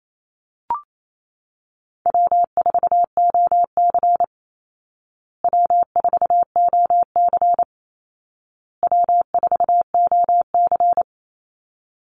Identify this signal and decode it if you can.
Morse code, heard as a keyed tone, about 28 wpm, 700 Hz. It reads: W4OC W4OC W4OC
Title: Easy Call Signs Repeated 3x - 28wpm